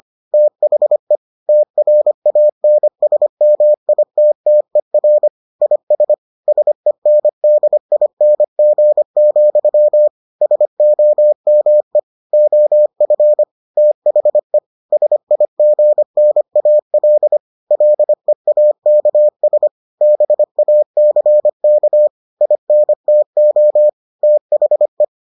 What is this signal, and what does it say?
Morse code, heard as a keyed tone, about 25 wpm, 600 Hz. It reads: THE TRANSMITTER IS SENDING, SOME OF THE SIGNAL LEAKS BACK INTO THE